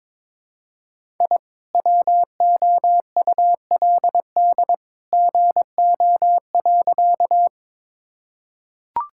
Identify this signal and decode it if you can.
Morse code, heard as a keyed tone, about 22 wpm, 700 Hz. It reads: I WOULD GO.
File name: practice / 22wpm / Sentences with 2-Word Phrases - 22wpm